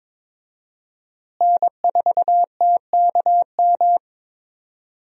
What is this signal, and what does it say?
Morse code, heard as a keyed tone, about 22 wpm, 700 Hz. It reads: N4TKM
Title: USA Call Signs - 5 Characters - Repeated 3x - 22wpm